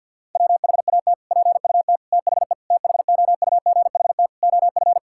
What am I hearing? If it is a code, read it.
Morse code, heard as a keyed tone, about 51 wpm, 690 Hz. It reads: WHAT PUT THE THOUGHT OF
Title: Sentences from Top 1000 Words - Review Part 5 - 50wpm